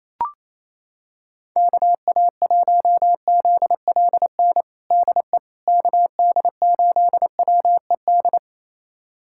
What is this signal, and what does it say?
Morse code, heard as a keyed tone, about 28 wpm, 700 Hz. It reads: KA1ZLN DE KD8WED
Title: USA Callsign DE Callsign - 6 Characters - 28wpm